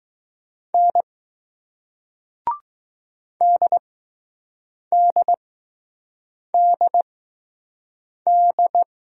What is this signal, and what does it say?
Morse code, heard as a keyed tone, about 20 wpm, spaced as for 6 wpm, 700 Hz. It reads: N DDDD